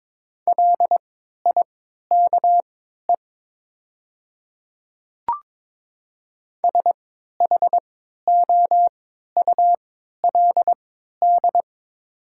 Morse code, heard as a keyed tone, about 22 wpm, spaced as for 12 wpm, 700 Hz. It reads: LIKE SHOULD
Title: 3x Character Spacing - Top 300 Words - 22wpm